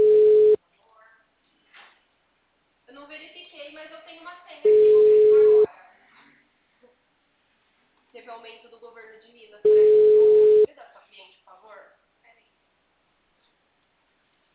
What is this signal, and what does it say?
Ringback tone (Europe, 425 Hz), with speech.
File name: out- -2012- - - .36237